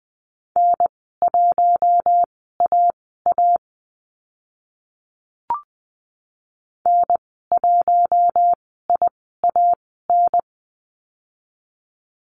Morse code, heard as a keyed tone, about 20 wpm, spaced as for 14 wpm, 700 Hz. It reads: N1AA N1IAN
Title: 062 Lesson 11g - Letter h - Call Signs 20wpm